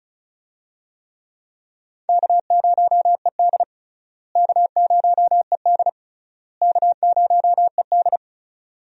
Morse code, heard as a keyed tone, about 34 wpm, 700 Hz. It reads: K0ED K0ED K0ED
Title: USA Call Signs - 4 Characters - Repeated 3x - 35wpm